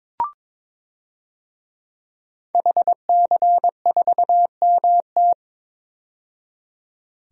Morse code, heard as a keyed tone, about 22 wpm, 700 Hz. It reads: HC4MT
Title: DXCC-100 Call Signs - 5 Characters - 22wpm